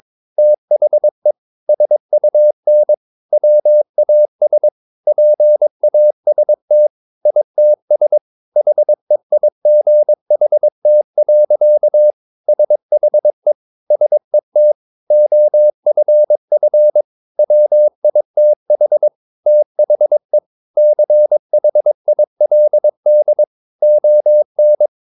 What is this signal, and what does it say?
Morse code, heard as a keyed tone, about 22 wpm, 600 Hz. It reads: THE SUN WAS PAST ITS HEIGHT. SHE SET OFF WITH THE CHILD ON
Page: Romanen Never af Ken Follet på engelsk.